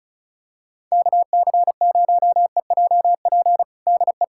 Morse code, heard as a keyed tone, about 34 wpm, 700 Hz. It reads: KC0EJP DE
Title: USA Callsign DE Callsign - 6 Characters - 35wpm